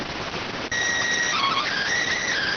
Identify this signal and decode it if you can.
SSTV (slow-scan television), PD 290.